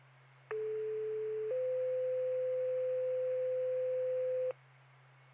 Two-tone page, heard as a one-second tone, then a three-second tone, about 430 then 510 Hz. It is heard from 0.5 to 4.5 s.